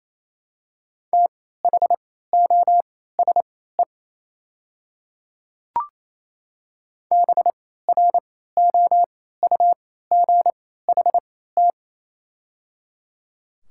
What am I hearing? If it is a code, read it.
Morse code, heard as a keyed tone, about 28 wpm, spaced as for 16 wpm, 700 Hz. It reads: THOSE BROUGHT
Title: 3x Character Spacing - Top 500 Words - 28wpm